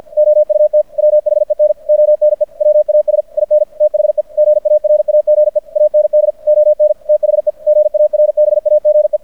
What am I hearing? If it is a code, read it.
Morse code, heard as a keyed tone, about 50 wpm, 600 Hz. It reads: OUT WHEN ONE WAS IN THE GARAGE AND ON THE GARBAGE
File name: archived / CRSnet